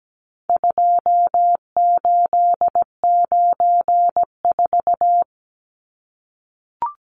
Morse code, heard as a keyed tone, about 17 wpm, 700 Hz. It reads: 2894
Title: Numbers - 4-Digit 17wpm